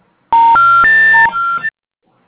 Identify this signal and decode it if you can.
Special information tone.